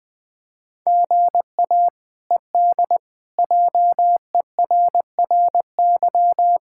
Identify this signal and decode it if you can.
Morse code, heard as a keyed tone, about 20 wpm, 700 Hz. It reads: GA ED JERRY
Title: K1USN SST QSO Practice - 20wpm